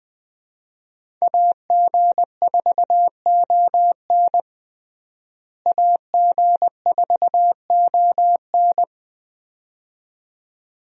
Morse code, heard as a keyed tone, about 20 wpm, 700 Hz. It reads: AG4ON AG4ON
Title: USA Call Signs - 5 Characters - Repeated 2x - 20wpm